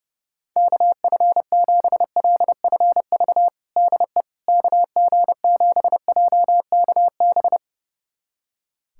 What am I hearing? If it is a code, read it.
Morse code, heard as a keyed tone, about 30 wpm, 700 Hz. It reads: KF7LFV DE KG7JKB
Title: USA Callsign DE Callsign 30wpm